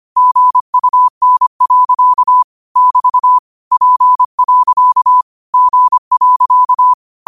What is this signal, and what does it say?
Morse code, heard as a keyed tone, about 25 wpm, 1000 Hz. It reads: GUN. <BT> P. G.